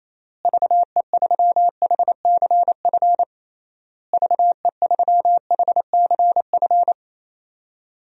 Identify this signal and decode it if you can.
Morse code, heard as a keyed tone, about 28 wpm, 700 Hz. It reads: VE3HCF VE3HCF
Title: DXCC-100 Call Signs - 6 Characters - Repeated 3x - 28wpm